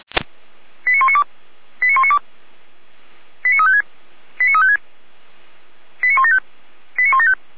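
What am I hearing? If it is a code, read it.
Five-tone selective call (ZVEI), heard as quick runs of five tones, tones of about 70 ms. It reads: 89192 89192 89356 89356 89175 89175
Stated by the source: ZVEI five-tone call 89337 received by SDR, recorded by Feuerwehr Orlamünde fire brigade.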